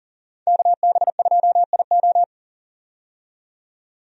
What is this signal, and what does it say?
Morse code, heard as a keyed tone, about 40 wpm, 700 Hz. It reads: KB2IO
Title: USA Call Signs - 5 Characters - 40wpm